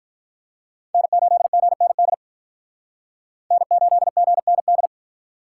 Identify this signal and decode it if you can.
Morse code, heard as a keyed tone, about 46 wpm, 700 Hz. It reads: N8GND N8GND